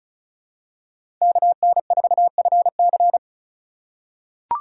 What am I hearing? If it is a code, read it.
Morse code, heard as a keyed tone, about 34 wpm, 690 Hz. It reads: KN4FC
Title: USA Call Signs - 5 Characters - 2x Character Spacing - 35wpm